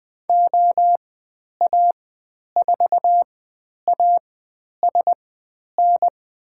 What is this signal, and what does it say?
Morse code, heard as a keyed tone, about 21 wpm, spaced as for 10 wpm, 700 Hz. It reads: OA4ASN
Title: 102 Lesson 17g - Letter u - Call Signs 20wpm